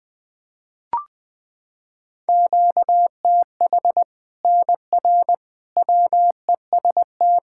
Morse code, heard as a keyed tone, about 20 wpm, 700 Hz. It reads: QTH NR WEST
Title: Practice QSO - Exchange 1 - Beginning - 20wpm